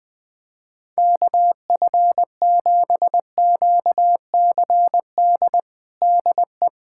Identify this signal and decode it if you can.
Morse code, heard as a keyed tone, about 20 wpm, 700 Hz. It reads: KF7QCD DE
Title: USA Callsign DE Callsign 20wpm